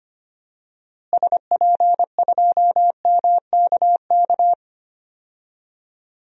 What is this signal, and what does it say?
Morse code, heard as a keyed tone, about 25 wpm, 700 Hz. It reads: SP2MKK
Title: DXCC-100 Call Signs - 6 Characters - Repeated 3x - 25wpm